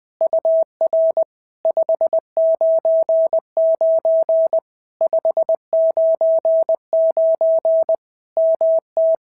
Morse code, heard as a keyed tone, about 20 wpm, 650 Hz. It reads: UR 599 599 MT
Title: POTA QSO Practice - 20wpm